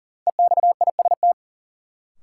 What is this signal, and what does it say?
Morse code, heard as a keyed tone, about 40 wpm, 700 Hz. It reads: EXIST
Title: Sentences from Top 1000 Words - Review Part 5 - New Word Drill - 40wpm